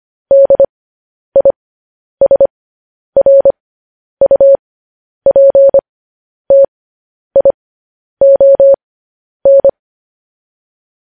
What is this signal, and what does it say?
Morse code, heard as a keyed tone, about 25 wpm, spaced as for 10 wpm, 550 Hz. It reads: DISRUPTION
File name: podcast / News headlines-1x-25x10